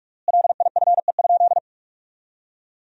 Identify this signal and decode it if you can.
Morse code, heard as a keyed tone, about 44 wpm, 700 Hz. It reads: LIFE?
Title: 3x Word-Spacing - Sentences from Top 300 Words - 45wpm